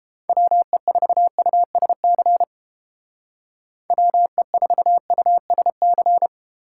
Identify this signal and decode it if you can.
Morse code, heard as a keyed tone, about 29 wpm, 700 Hz. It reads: WE4USC WE4USC